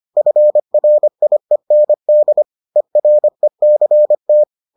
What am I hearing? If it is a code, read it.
Morse code, heard as a keyed tone, about 25 wpm, 600 Hz. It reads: FRIEND ERECT